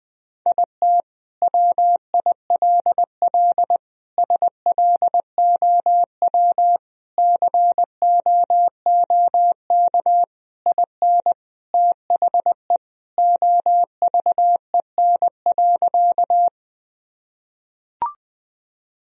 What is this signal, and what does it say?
Morse code, heard as a keyed tone, about 20 wpm, 700 Hz. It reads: IT WILL SLOW COOK IN THE OVEN.
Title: Sentences from Top 1000 Words - Review Part 4 - 20wpm